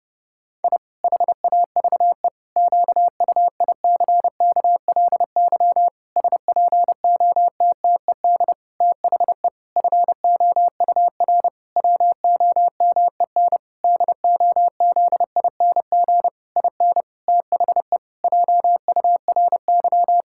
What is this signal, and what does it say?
Morse code, heard as a keyed tone, about 30 wpm, 700 Hz. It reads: I HAVE QUICKLY SPOTTED THE FOUR WOMEN DOZING IN THE JURY
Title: Pangram Challenge v2 - 30wpm